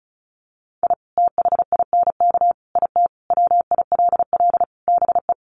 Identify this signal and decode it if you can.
Morse code, heard as a keyed tone, about 34 wpm, 700 Hz. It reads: I THINK IT WILL BE
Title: Sets of 5 Words Encore2 35wpm